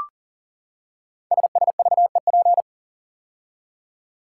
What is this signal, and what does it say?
Morse code, heard as a keyed tone, about 42 wpm, 700 Hz. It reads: SSVEP
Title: Code Groups - 5 Letters - 40wpm